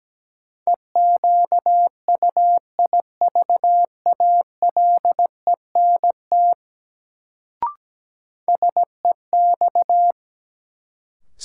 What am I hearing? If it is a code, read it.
Morse code, heard as a keyed tone, about 17 wpm, 700 Hz. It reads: EQUIVALENT SEX
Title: Sentences from Top 1600 Words - New Word Drill Alt - 17wpm